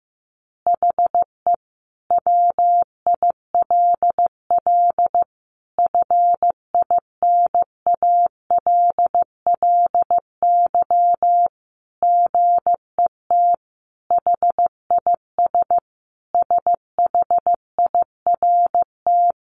Morse code, heard as a keyed tone, about 15 wpm, 700 Hz. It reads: HE WILL FINALLY GET HIS SHIRT
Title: Additional Easy Phrases 15wpm